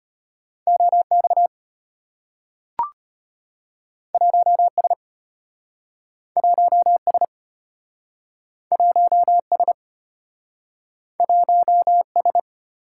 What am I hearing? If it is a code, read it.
Morse code, heard as a keyed tone, about 34 wpm, 700 Hz. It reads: OX 1S 1S 1S 1S